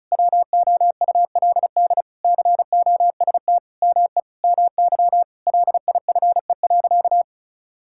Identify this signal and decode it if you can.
Morse code, heard as a keyed tone, about 36 wpm, 700 Hz. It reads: WOULD COST ME MY LIFE.
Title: Sentences from Top 1000 Words - Review Part 5 - 35wpm